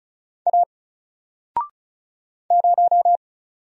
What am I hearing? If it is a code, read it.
Morse code, heard as a keyed tone, about 36 wpm, 700 Hz. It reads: A 0
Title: Single Letter-Number Alt 35wpm